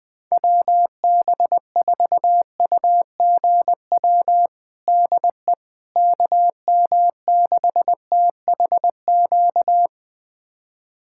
Morse code, heard as a keyed tone, about 20 wpm, 700 Hz. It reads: WB4UGW DE KM6THQ
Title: USA Callsign DE Callsign - 6 Characters - 20wpm